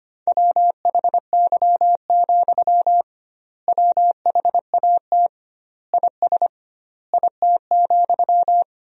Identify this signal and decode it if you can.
Morse code, heard as a keyed tone, about 25 wpm, 700 Hz. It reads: WHY, WHAT IS IT,
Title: 2x Word-Spacing - Sentences from Top 200 Words - 25wpm